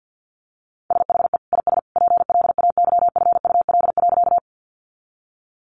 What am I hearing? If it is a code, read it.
Morse code, heard as a keyed tone, about 50 wpm, 700 Hz. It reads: SHE IS PLAYFUL.
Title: Easy Phrases 50wpm